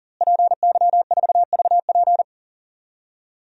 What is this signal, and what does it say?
Morse code, heard as a keyed tone, about 40 wpm, 700 Hz. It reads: PY4VP
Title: DXCC-100 Call Signs - 5 Characters - Repeated 3x - 40wpm